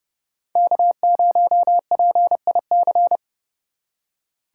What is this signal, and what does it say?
Morse code, heard as a keyed tone, about 29 wpm, 700 Hz. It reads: K0PIC